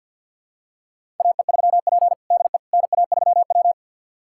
Morse code, heard as a keyed tone, about 52 wpm, 700 Hz. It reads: AE3P DE NA3W
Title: USA Callsign DE Callsign - 4 Characters - 50wpm